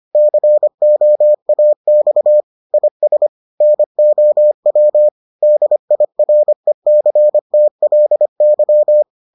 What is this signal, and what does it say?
Morse code, heard as a keyed tone, about 25 wpm, 600 Hz. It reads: COAX IS NOW DIRECTLY